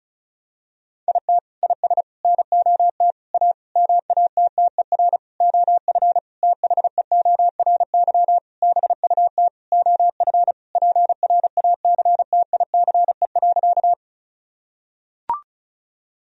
Morse code, heard as a keyed tone, about 36 wpm, 700 Hz. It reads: IT IS NOT A MATTER OF THEORY BUT OF PRACTICE.
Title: Sentences from Top 1000 Words - Review Part 4 - 35wpm